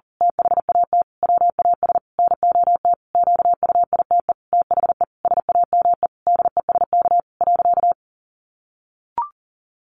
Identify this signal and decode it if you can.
Morse code, heard as a keyed tone, about 40 wpm, 700 Hz. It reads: THAT WAS NOT QUITE THE SAME DESK.
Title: A Sound Of Thunder 40wpm